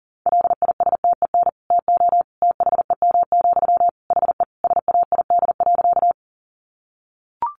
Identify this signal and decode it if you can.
Morse code, heard as a keyed tone, about 40 wpm, 700 Hz. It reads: LISTEN TO THEM, HE SAID.
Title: Sentences from Top 400 Words 40wpm